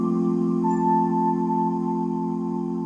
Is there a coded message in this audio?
No.